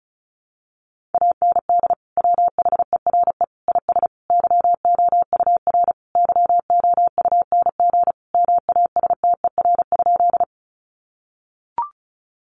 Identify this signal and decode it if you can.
Morse code, heard as a keyed tone, about 34 wpm, 700 Hz. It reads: AND WHERE IS YOUR YOUNG MASTER?
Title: Sentences from Top 1300 Words 35wpm